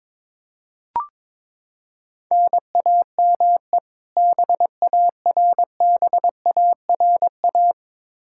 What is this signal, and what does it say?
Morse code, heard as a keyed tone, about 22 wpm, 700 Hz. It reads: NAME BARBARA
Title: Practice QSO - Exchange 1 - Beginning - 22wpm